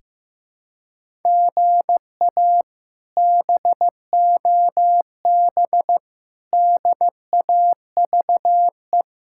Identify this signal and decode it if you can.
Morse code, heard as a keyed tone, about 15 wpm, 700 Hz. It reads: GA BOB DAVE
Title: K1USN SST QSO Practice - 15wpm